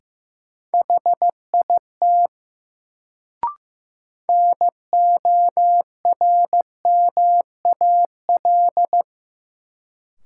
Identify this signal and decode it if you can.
Morse code, heard as a keyed tone, about 15 wpm, 700 Hz. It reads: HIT NORMAL
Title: Sentences from Top 1200 Words - New Word Drill - 15wpm